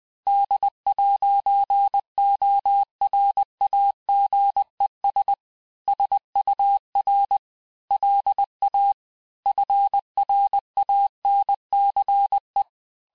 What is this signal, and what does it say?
Morse code, heard as a keyed tone, about 20 wpm, 800 Hz. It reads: D'ORAGES SUR LA FRANCE
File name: cours-du-mercredi / audios / cours-858-Fin-de-QSO-vitesse-19mpm